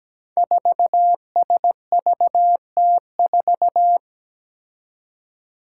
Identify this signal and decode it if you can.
Morse code, heard as a keyed tone, about 17 wpm, 700 Hz. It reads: 4SVT4
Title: Five Challenging Letter-Number Combinations 17wpm